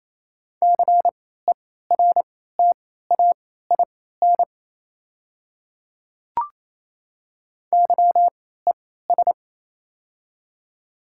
Morse code, heard as a keyed tone, about 28 wpm, spaced as for 16 wpm, 700 Hz. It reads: CERTAIN YES
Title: 3x Character Spacing - Top 500 Words - 28wpm